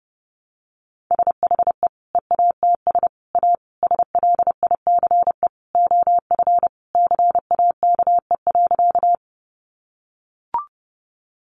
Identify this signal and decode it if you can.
Morse code, heard as a keyed tone, about 30 wpm, 700 Hz. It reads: SHE EATS A SLICE OF CAKE.
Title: Easy Phrases 30wpm